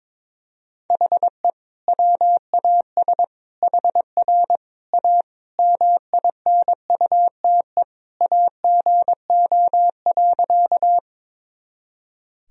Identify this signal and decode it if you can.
Morse code, heard as a keyed tone, about 22 wpm, 700 Hz. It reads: HE WAS HR A MINUTE AGO.
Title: Sentences from Top 500 Words and Abbreviations 22wpm